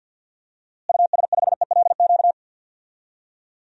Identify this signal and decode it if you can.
Morse code, heard as a keyed tone, about 52 wpm, 700 Hz. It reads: US5ELQ
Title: DXCC-100 Call Signs - 6 Characters - Repeated 2x - 50wpm